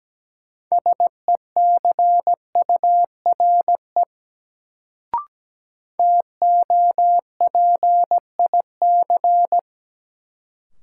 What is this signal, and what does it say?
Morse code, heard as a keyed tone, about 17 wpm, 700 Hz. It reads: SECURE TOPIC
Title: Sentences from Top 1200 Words - New Word Drill - 17wpm